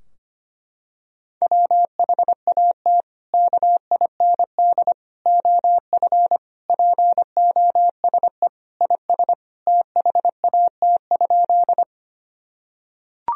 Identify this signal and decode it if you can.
Morse code, heard as a keyed tone, about 25 wpm, 700 Hz. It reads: WHAT KIND OF POSE IS THAT?
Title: Sentences from Top 1000 Words - Review Part 2 - 25wpm